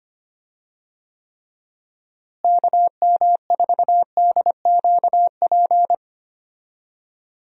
Morse code, heard as a keyed tone, about 25 wpm, 700 Hz. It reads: KM4DQP
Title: USA Call Signs - 6 Characters - Repeated 3x - 25wpm